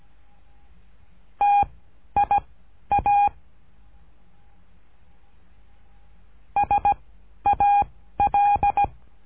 Morse code, heard as a keyed tone, about 17 wpm, 810 Hz. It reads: T I A S A L
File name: cursos / CURSO DE TELEGRAFIA / Telegrafia - Trilha 13